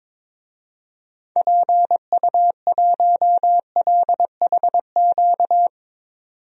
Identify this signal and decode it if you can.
Morse code, heard as a keyed tone, about 22 wpm, 700 Hz. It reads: PU1LHQ
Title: DXCC-100 Call Signs - 6 Characters - 22wpm